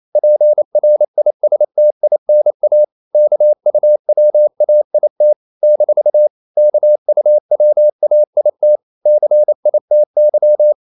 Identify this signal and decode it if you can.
Morse code, heard as a keyed tone, about 28 wpm, 600 Hz. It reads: PRISTINA KUWAIT - KUWAIT CITY
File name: CW / Practice / NationalCapitals28